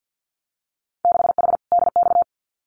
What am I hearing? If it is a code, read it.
Morse code, heard as a keyed tone, about 50 wpm, 700 Hz. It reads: 6H DX
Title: Winter Field Day Exchanges Encore 50wpm